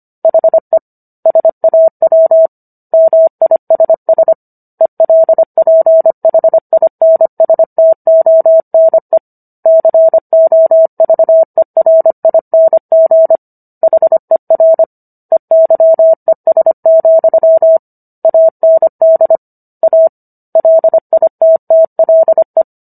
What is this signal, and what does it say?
Morse code, heard as a keyed tone, about 25 wpm, 650 Hz. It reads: HE SAW MISS ELPHINSTONE COVERING HER EYES, AND A LITTLE